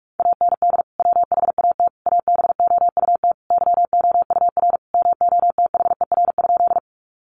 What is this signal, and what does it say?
Morse code, heard as a keyed tone, about 44 wpm, 700 Hz. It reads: AND WHAT ABOUT YOUR MOTHER?
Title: Sentences from Top 200 Words 45wpm